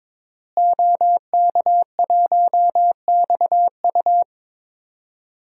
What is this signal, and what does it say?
Morse code, heard as a keyed tone, about 22 wpm, 700 Hz. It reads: OK1XU
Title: DXCC-100 Call Signs - 5 Characters - Repeated 2x - 22wpm